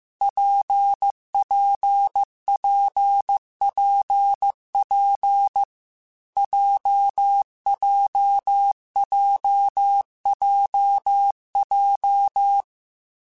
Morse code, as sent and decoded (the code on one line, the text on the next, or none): .--. .--. .--. .--. .--. / .--- .--- .--- .--- .---
PPPPP JJJJJ